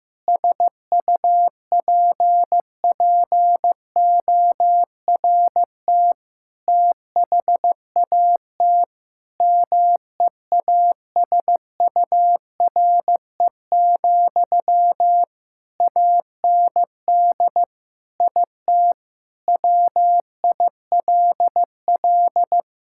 Morse code, heard as a keyed tone, about 15 wpm, 700 Hz. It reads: SUPPORT THAT MEASURE, AND IT WILL
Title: Sentences from Top 1000 Words - Review Part 5 - 15wpm